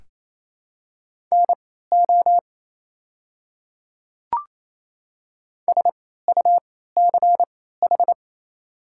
Morse code, heard as a keyed tone, about 27 wpm, spaced as for 16 wpm, 700 Hz. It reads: NO SUCH
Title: 3x Character Spacing - Top 200 Words - 28wpm